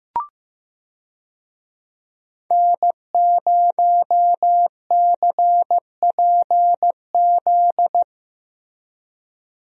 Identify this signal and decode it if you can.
Morse code, heard as a keyed tone, about 15 wpm, 700 Hz. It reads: N0CPZ
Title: USA Call Signs - 5 Characters - Repeated 3x - 15wpm